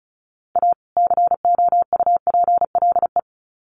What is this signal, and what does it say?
Morse code, heard as a keyed tone, about 34 wpm, 700 Hz. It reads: A COUPLE
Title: Sets of 2 Words 35wpm